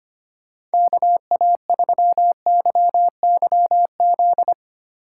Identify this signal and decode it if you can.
Morse code, heard as a keyed tone, about 25 wpm, 700 Hz. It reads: KA3YYZ
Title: USA Call Signs - 6 Characters - 4x Character Spacing - 25wpm